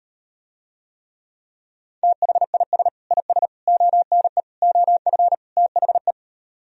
Morse code, heard as a keyed tone, about 38 wpm, 700 Hz. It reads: THIS IS ONE OF THE